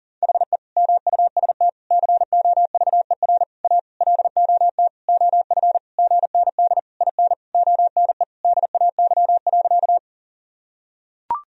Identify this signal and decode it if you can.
Morse code, heard as a keyed tone, about 40 wpm, 700 Hz. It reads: HE MUST COVER A LOT OF GND IN ONE DAY.
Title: Sentences from Top 500 Words and Abbreviations 40wpm